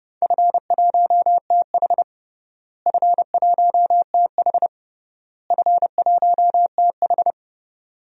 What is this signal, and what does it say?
Morse code, heard as a keyed tone, about 29 wpm, 700 Hz. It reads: F1TH F1TH F1TH
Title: DXCC-100 Call Signs - 4 Characters - Repeated 3x - 30wpm